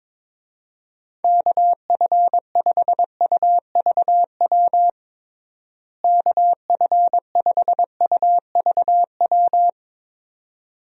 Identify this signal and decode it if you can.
Morse code, heard as a keyed tone, about 22 wpm, 700 Hz. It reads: KF5UVW KF5UVW
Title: USA Call Signs - 6 Characters - Repeated 2x - 22wpm